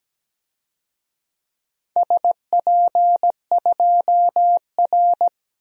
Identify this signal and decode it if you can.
Morse code, heard as a keyed tone, about 17 wpm, 700 Hz. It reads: SP2R
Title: DXCC-100 Call Signs - 4 Characters - 17wpm